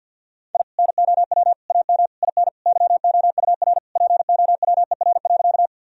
Morse code, heard as a keyed tone, about 51 wpm, 700 Hz. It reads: I NOW AM IN YOUR POWER.
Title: Sentences from Top 1000 Words - Review Part 2 - 50wpm